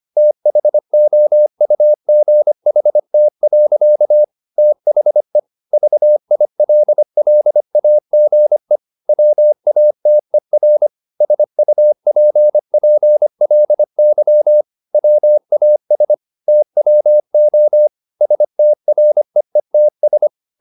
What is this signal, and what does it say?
Morse code, heard as a keyed tone, about 25 wpm, 600 Hz. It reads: THOUGHT. THE VILLAGE WATER SUPPLY WAS TWO STREETS
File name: the old man and the sea part 1 25wpm